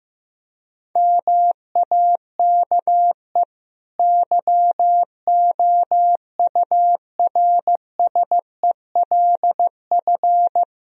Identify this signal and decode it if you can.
Morse code, heard as a keyed tone, about 15 wpm, 700 Hz. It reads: MAKE YOURSELF
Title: Sentences from Top 1000 Words - Review Part 3 - 15wpm